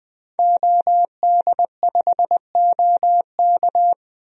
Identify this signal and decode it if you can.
Morse code, heard as a keyed tone, about 20 wpm, 700 Hz. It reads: OD5OK
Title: DXCC-100 Call Signs - 5 Characters - 20wpm